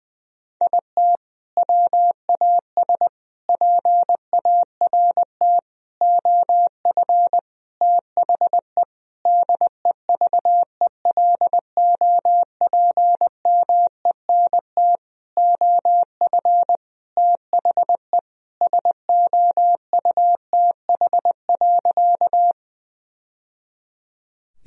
Morse code, heard as a keyed tone, about 20 wpm, 700 Hz. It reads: IT WAS PART OF THE DEVELOPMENT OF THE SOUTH.
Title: Sentences from Top 1000 Words - Review Part 4 - 20wpm